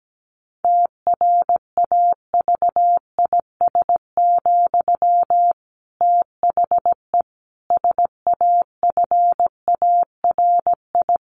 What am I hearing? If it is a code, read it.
Morse code, heard as a keyed tone, about 17 wpm, 700 Hz. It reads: TRAVIS, THE SAFARI
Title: A Sound Of Thunder 17wpm